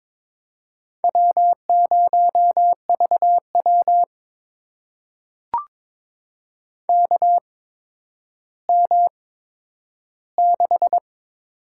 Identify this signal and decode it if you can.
Morse code, heard as a keyed tone, about 22 wpm, spaced as for 11 wpm, 700 Hz. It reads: W0VW KM6
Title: USA Call Signs - 8x Character Spacing - 22wpm